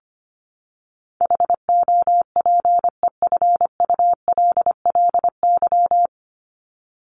Morse code, heard as a keyed tone, about 25 wpm, 700 Hz. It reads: HOPEFULLY
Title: Top 1251-1500 Words 25wpm